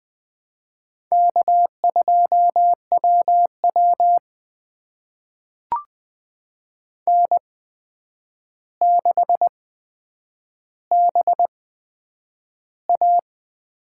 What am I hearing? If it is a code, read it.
Morse code, heard as a keyed tone, about 20 wpm, spaced as for 9 wpm, 700 Hz. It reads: K2WW N6BA
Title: USA Call Signs - 4 Characters - 8x Character Spacing - 20wpm